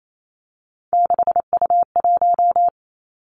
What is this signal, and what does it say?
Morse code, heard as a keyed tone, about 28 wpm, 700 Hz. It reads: 6U1
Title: Three Challenging Letter-Number Combinations 28wpm